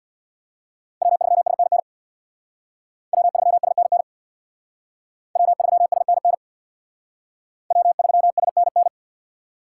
Morse code, heard as a keyed tone, about 60 wpm, 700 Hz. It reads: W3SNN W3SNN W3SNN W3SNN
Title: Speed Racing - USA 1x3 Call Signs 50wpm